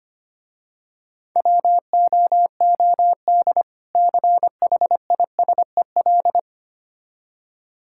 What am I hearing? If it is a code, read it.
Morse code, heard as a keyed tone, about 25 wpm, 700 Hz. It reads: WOOD CHISEL
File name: practice / 25wpm / Difficult Words - Speed-Racing with Hints - 25wpm